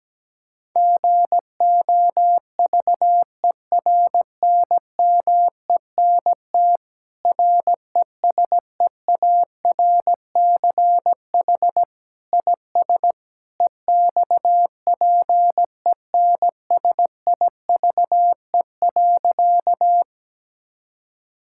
Morse code, heard as a keyed tone, about 17 wpm, 700 Hz. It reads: GOVERNMENT RESEARCH IS EXPENSIVE.
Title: Sentences from Top 1000 Words - Review Part 6 - 17wpm